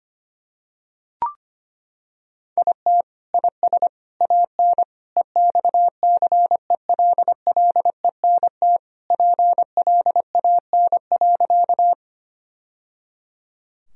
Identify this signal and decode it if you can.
Morse code, heard as a keyed tone, about 25 wpm, 700 Hz. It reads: IT IS AN EXCELLENT PLAN.
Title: Sentences from Top 1000 Words - Review Part 2 - 25wpm